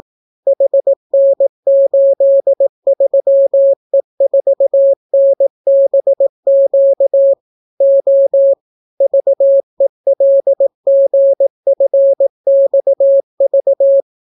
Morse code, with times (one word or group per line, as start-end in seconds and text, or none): 0.5-7.3 HN83E4NBQ
7.8-8.5 O
9.0-14.0 VELGFXV